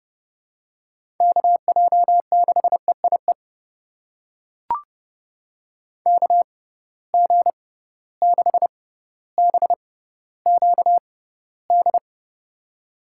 Morse code, heard as a keyed tone, about 30 wpm, spaced as for 15 wpm, 700 Hz. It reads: KJ6EIE KG6BQD
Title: USA Call Signs - 6x Character Spacing - 30wpm